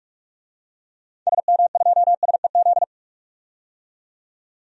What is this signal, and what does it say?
Morse code, heard as a keyed tone, about 43 wpm, 700 Hz. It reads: SM2SEZ
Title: DXCC-100 Call Signs - 6 Characters - 45wpm